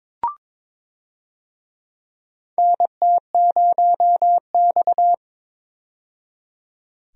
Morse code, heard as a keyed tone, about 22 wpm, 700 Hz. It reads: NT0X
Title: USA Call Signs - 4 Characters - 22wpm